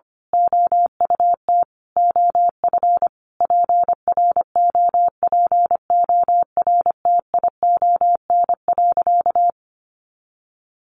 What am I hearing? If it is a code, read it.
Morse code, heard as a keyed tone, about 25 wpm, 700 Hz. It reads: OUT OF PROPORTION.
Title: A Sound Of Thunder 25wpm